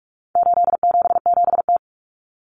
Morse code, heard as a keyed tone, about 44 wpm, 700 Hz. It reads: 877T
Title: Common Cut Numbers - 4-Digit 45wpm